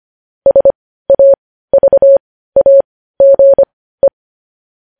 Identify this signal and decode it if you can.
Morse code, heard as a keyed tone, about 25 wpm, spaced as for 15 wpm, 550 Hz. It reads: SAVAGE
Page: news headlines in morse code at 15 wpm